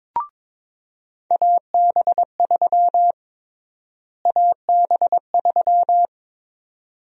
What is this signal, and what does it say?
Morse code, heard as a keyed tone, about 21 wpm, 700 Hz. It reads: AB3 AB3
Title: USA Call Sign Prefixes - Repeated 2x - 22wpm